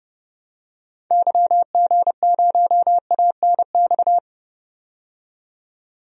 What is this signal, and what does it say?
Morse code, heard as a keyed tone, about 30 wpm, 700 Hz. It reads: YG0ANX
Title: DXCC-100 Call Signs - 6 Characters - 30wpm